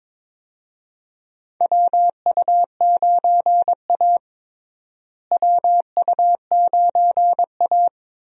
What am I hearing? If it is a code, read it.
Morse code, heard as a keyed tone, about 22 wpm, 700 Hz. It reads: WU9A WU9A
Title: USA Call Signs - 4 Characters - Repeated 2x - 22wpm